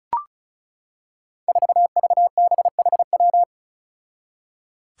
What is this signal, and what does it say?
Morse code, heard as a keyed tone, about 36 wpm, 700 Hz. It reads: 4VBHW
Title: Five Challenging Letter-Number Combinations 35wpm